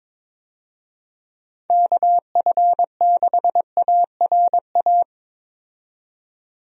Morse code, heard as a keyed tone, about 22 wpm, 700 Hz. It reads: KF6ARA
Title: USA Call Signs - 6 Characters - 22wpm